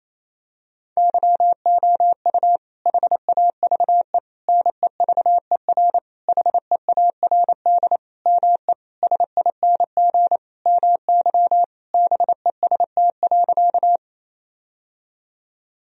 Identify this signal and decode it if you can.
Morse code, heard as a keyed tone, about 28 wpm, 700 Hz. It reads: YOU HAVE NEVER HEARD ME SING MY BEST.
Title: Sentences from Top 1000 Words - Review Part 2 - 28wpm